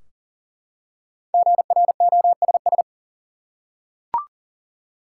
Morse code, heard as a keyed tone, about 40 wpm, 700 Hz. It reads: GROSS
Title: Five Letter Words - 40wpm